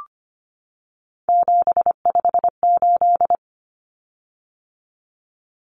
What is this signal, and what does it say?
Morse code, heard as a keyed tone, about 25 wpm, 700 Hz. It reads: 758
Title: Common Cut Numbers - 3-Digit 25wpm